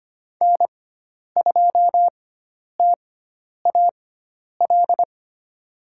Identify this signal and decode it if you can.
Morse code, heard as a keyed tone, about 25 wpm, spaced as for 10 wpm, 700 Hz. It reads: N2TAL
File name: Lesson 16g - Letter c - Call Signs 25wpm